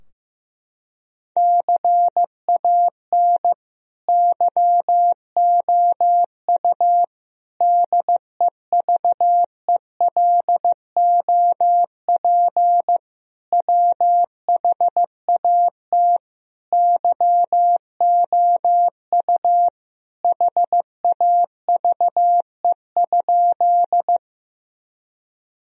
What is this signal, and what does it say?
Morse code, heard as a keyed tone, about 16 wpm, 700 Hz. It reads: CAN YOU DEVELOP WHAT YOU HAVE?
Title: Sentences from Top 1000 Words - Review Part 2 - 15wpm